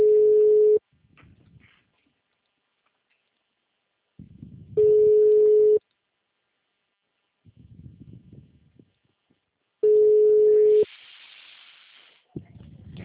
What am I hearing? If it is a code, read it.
Ringback tone (Europe, 425 Hz).